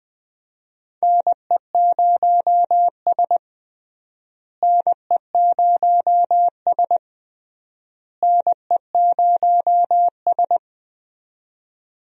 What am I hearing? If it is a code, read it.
Morse code, heard as a keyed tone, about 20 wpm, 700 Hz. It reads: NE0S NE0S NE0S
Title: USA Call Signs - 4 Characters - Repeated 3x - 20wpm